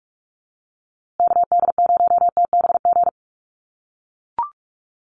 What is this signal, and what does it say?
Morse code, heard as a keyed tone, about 44 wpm, 700 Hz. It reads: KD0TBG
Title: USA Call Signs Repeated 3x 45wpm